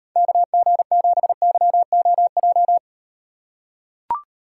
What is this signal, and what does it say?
Morse code, heard as a keyed tone, about 38 wpm, 700 Hz. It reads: KG7YOJ